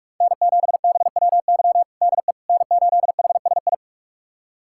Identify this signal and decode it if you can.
Morse code, heard as a keyed tone, about 46 wpm, 700 Hz. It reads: N7BWY DE N8HSI
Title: USA Callsign DE Callsign - 5 Characters - 45wpm